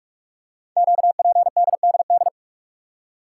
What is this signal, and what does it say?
Morse code, heard as a keyed tone, about 44 wpm, 700 Hz. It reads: QPDDD
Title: Code Groups - 5 Letters - 45wpm